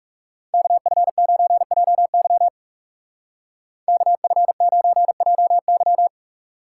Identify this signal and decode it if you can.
Morse code, heard as a keyed tone, about 36 wpm, 700 Hz. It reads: KF9JY KF9JY
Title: Speed Racing - USA 2x2 Call Signs 30wpm